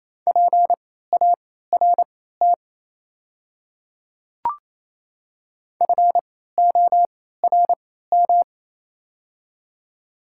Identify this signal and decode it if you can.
Morse code, heard as a keyed tone, about 28 wpm, spaced as for 16 wpm, 700 Hz. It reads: PART FORM
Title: 3x Character Spacing - Top 200 Words - 28wpm